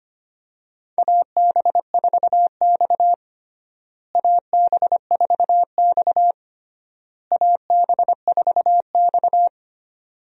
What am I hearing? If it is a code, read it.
Morse code, heard as a keyed tone, about 25 wpm, 700 Hz. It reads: AB4X AB4X AB4X
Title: USA Call Signs - 4 Characters - Repeated 3x - 25wpm